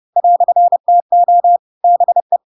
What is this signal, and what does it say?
Morse code, heard as a keyed tone, about 30 wpm, 700 Hz. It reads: "TO BE
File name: Pinocchio / Pinocchio ch8 30 30 ik1hge